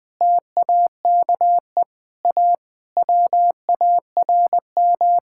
Morse code, heard as a keyed tone, about 21 wpm, 700 Hz. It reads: TAKE A WARM
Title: Sentences from Top 1600 Words 20wpm